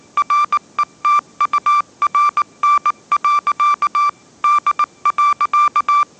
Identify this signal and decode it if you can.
Morse code, heard as a keyed tone, about 19 wpm, 1180 Hz. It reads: RETURN. D.
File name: mrx083 20wpm